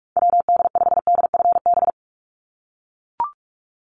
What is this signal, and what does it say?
Morse code, heard as a keyed tone, about 44 wpm, 700 Hz. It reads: WD5DFB
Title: USA Call Signs 45wpm